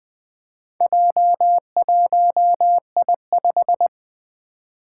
Morse code, heard as a keyed tone, about 21 wpm, 700 Hz. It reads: J1I5
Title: Four Challenging Letter-Number Combinations 20wpm